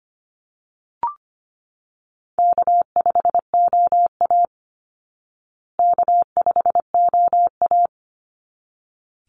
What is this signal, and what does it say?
Morse code, heard as a keyed tone, about 25 wpm, 700 Hz. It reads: K5OA K5OA
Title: Easy Call Signs Repeated 2x - 25wpm